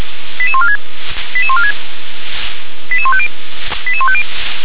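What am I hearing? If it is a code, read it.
Five-tone selective call (ZVEI), heard as quick runs of five tones, tones of about 70 ms. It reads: 88146 88146 88150 88150
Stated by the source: ZVEI five-tone selective call received on an SDR.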